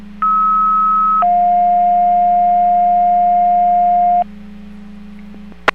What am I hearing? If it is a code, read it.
Two-tone page, heard as a one-second tone, then a three-second tone, about 1290 then 710 Hz. It is heard from 0.2 to 4.2 s.